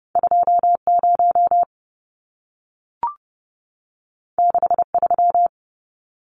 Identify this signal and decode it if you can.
Morse code, heard as a keyed tone, about 30 wpm, 700 Hz. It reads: 20 63
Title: Numbers - 2-Digit 30wpm